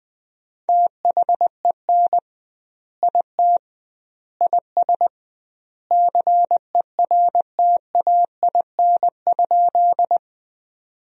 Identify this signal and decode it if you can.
Morse code, heard as a keyed tone, about 21 wpm, 700 Hz. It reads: THEN IT IS CERTAIN?
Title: 2x Word-Spacing - Sentences from Top 500 Words - 20wpm